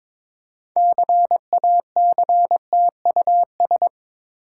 Morse code, heard as a keyed tone, about 22 wpm, 690 Hz. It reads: CACTUS
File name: Difficult Words with Hints - 22wpm